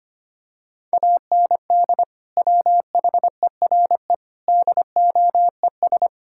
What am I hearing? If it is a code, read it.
Morse code, heard as a keyed tone, about 25 wpm, 700 Hz. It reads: AND WHERE DOES
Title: Sentences from Top 1000 Words - Review Part 5 - 25wpm